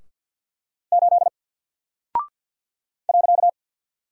Morse code, heard as a keyed tone, about 48 wpm, 700 Hz. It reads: 8 .
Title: Single Character (Letter:Number:Punctuation:Prosign) 50wpm